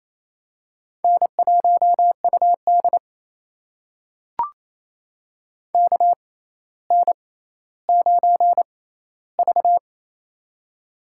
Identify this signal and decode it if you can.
Morse code, heard as a keyed tone, about 27 wpm, spaced as for 14 wpm, 700 Hz. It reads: N1UD KN9V
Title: USA Call Signs - 4 Characters - 6x Character Spacing - 28wpm